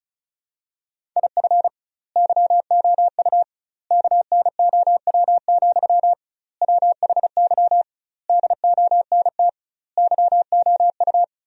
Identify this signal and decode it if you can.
Morse code, heard as a keyed tone, about 34 wpm, 700 Hz. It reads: IF YOU KNOW, WHY DONT YOU
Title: 2x Word-Spacing - Sentences from Top 300 Words - 35wpm